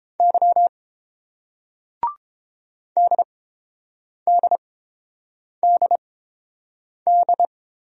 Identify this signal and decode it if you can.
Morse code, heard as a keyed tone, about 30 wpm, spaced as for 7 wpm, 700 Hz. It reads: Y DDDD